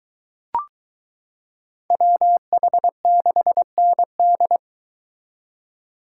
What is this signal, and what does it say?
Morse code, heard as a keyed tone, about 22 wpm, 700 Hz. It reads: WH6ND